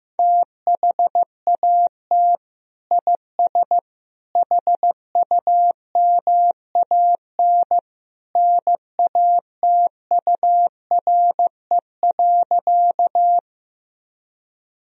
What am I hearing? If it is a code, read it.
Morse code, heard as a keyed tone, about 15 wpm, 700 Hz. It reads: THAT IS HUMAN NATURE.
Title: Sentences from Top 1000 Words - Review Part 3 - 15wpm